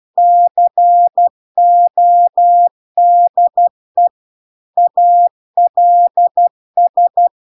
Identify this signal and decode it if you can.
Morse code, heard as a keyed tone, about 12 wpm, 700 Hz. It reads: CODE ALS
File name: cw-cursus-VERON-a03